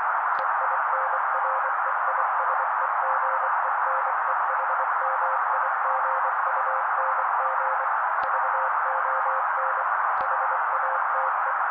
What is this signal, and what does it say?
Morse code, heard as a keyed tone, about 24 wpm, 530 Hz. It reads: EINREISEGENEHMIGUNG VON HATE
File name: daily / 23wpm noise